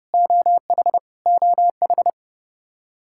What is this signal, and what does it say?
Morse code, heard as a keyed tone, about 30 wpm, 700 Hz. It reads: OH OH
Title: Practice QSO - Exchange 1 - Beginning - 30wpm